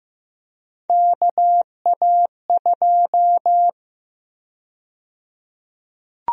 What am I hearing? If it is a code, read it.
Morse code, heard as a keyed tone, about 15 wpm, 700 Hz. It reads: KA2
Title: USA Call Sign Prefixes - 15wpm